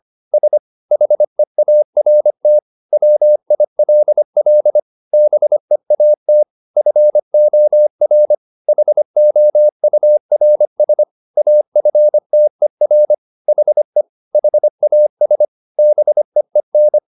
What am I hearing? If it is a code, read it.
Morse code, heard as a keyed tone, about 25 wpm, 600 Hz. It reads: S HEART WILL BEAT FOR HOURS AFTER HE HAS BEEN
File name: the old man and the sea part 1 25wpm